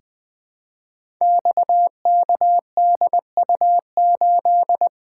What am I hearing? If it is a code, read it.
Morse code, heard as a keyed tone, about 20 wpm, 700 Hz. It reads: XKDU8
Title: Code Groups - 5 Letter:Numbers - 20wpm